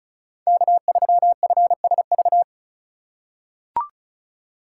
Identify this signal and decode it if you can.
Morse code, heard as a keyed tone, about 36 wpm, 700 Hz. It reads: K3FSV
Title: USA Call Signs - 5 Characters - 2x Character Spacing - 35wpm